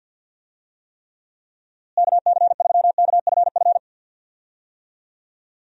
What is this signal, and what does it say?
Morse code, heard as a keyed tone, about 49 wpm, 700 Hz. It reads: KC3KFF
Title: USA Call Signs - 6 Characters - Repeated 2x - 50wpm